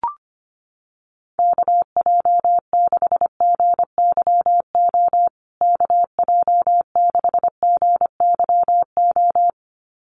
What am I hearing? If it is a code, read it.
Morse code, heard as a keyed tone, about 25 wpm, 700 Hz. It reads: KJ6GYO KJ6GYO
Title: USA Call Signs Repeated 2x 25wpm